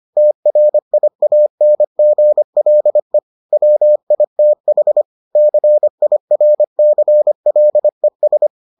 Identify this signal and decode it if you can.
Morse code, heard as a keyed tone, about 25 wpm, 600 Hz. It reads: TRIANGLE WITH CIRCLES